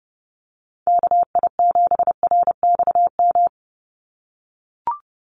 Morse code, heard as a keyed tone, about 28 wpm, 700 Hz. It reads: KI7RXM
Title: USA Call Signs 30wpm